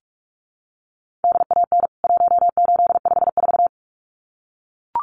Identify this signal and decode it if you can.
Morse code, heard as a keyed tone, about 44 wpm, 700 Hz. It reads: DAN 1854
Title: CWT Exchange 45wpm